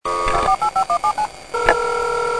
DTMF, heard as telephone touch tones, tones of about 90 ms. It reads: 48517
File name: music / ringtones / modem